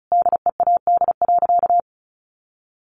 Morse code, heard as a keyed tone, about 34 wpm, 700 Hz. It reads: DEAD.
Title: A Sound Of Thunder 35wpm